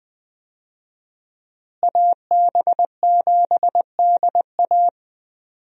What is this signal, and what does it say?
Morse code, heard as a keyed tone, about 21 wpm, 700 Hz. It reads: AB7DA
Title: USA Call Signs - 5 Characters - Repeated 3x - 20wpm